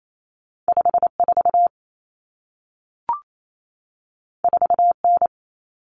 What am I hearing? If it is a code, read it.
Morse code, heard as a keyed tone, about 28 wpm, 700 Hz. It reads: 54 4N
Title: Common Cut Numbers - 2-Digit 28wpm